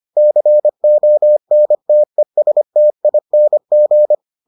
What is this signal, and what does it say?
Morse code, heard as a keyed tone, about 25 wpm, 600 Hz. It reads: CONTESTING